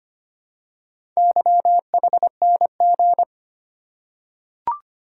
Morse code, heard as a keyed tone, about 25 wpm, 700 Hz. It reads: YHNG
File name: practice / 25wpm / Four Letter-Number Combinations - 25wpm